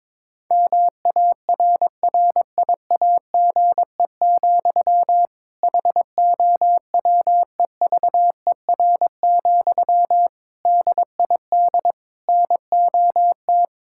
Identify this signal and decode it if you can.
Morse code, heard as a keyed tone, about 22 wpm, 700 Hz. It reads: MARRIAGE, HOWEVER, DID NOT
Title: Sentences from Top 1000 Words - Review Part 5 - 22wpm